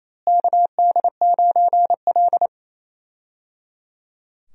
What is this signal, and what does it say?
Morse code, heard as a keyed tone, about 28 wpm, 700 Hz. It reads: KD9L
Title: USA Call Signs - 4 Characters - Repeated 3x - 28wpm